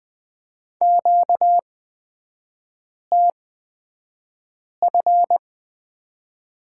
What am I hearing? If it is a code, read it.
Morse code, heard as a keyed tone, about 20 wpm, spaced as for 5 wpm, 700 Hz. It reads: QTF
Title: 236 Lesson 38e - Letter q - Three Characters 20wpm